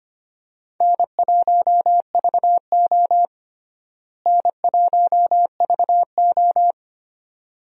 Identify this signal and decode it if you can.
Morse code, heard as a keyed tone, about 25 wpm, 700 Hz. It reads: N1VO N1VO
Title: USA Call Signs - 4 Characters - Repeated 2x - 25wpm